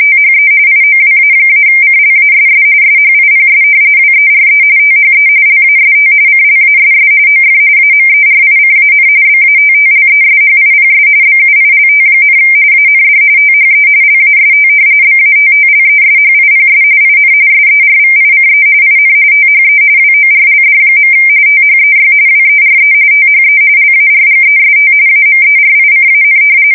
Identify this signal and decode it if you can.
RTTY, about 45.45 baud. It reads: THE IOWA-CLASS SHIPS WERE THE LAST CLASS OF AMERICAN BATTLESHIPS BUILT, AND THE LAST BATTLESHIPS IN SERVICE IN THE WORLD. THEY WERE THE LARGEST AMERICAN
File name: fsk170 / battleshipnj